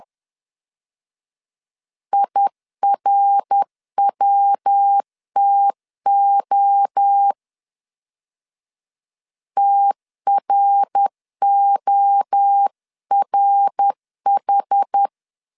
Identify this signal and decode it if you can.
Morse code, heard as a keyed tone, about 10 wpm, 800 Hz. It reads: IRWTO TRORH
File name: CW / DET / 1 3 1500C S C 10wpm 3335